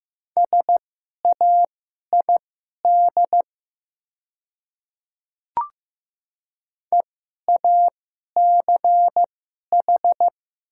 Morse code, heard as a keyed tone, about 15 wpm, spaced as for 11 wpm, 700 Hz. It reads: SAID EACH
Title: 2x Character Spacing - Top 200 Words - 15wpm